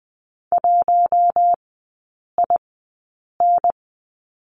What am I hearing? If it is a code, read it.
Morse code, heard as a keyed tone, about 20 wpm, spaced as for 8 wpm, 700 Hz. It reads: 1IN
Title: 041 Lesson 8e - Number 1 - Three Characters 20wpm